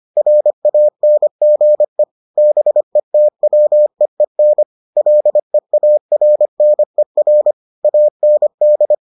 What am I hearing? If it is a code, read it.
Morse code, heard as a keyed tone, about 25 wpm, 600 Hz. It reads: RANGE BETWEEN LEARNER AND